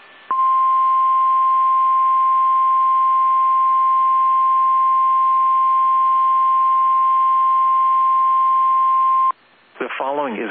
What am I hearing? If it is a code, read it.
Emergency Alert System (SAME) tones (the 1050 Hz attention tone), with speech. It is heard from 0.3 to 9.3 s.
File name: eas / KSDZKDJL / KSDZKDJL